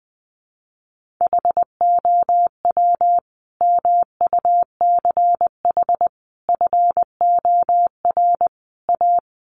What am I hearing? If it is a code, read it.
Morse code, heard as a keyed tone, about 20 wpm, 700 Hz. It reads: HOW MUCH FOR A
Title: Sentences from Top 1400 Words 20wpm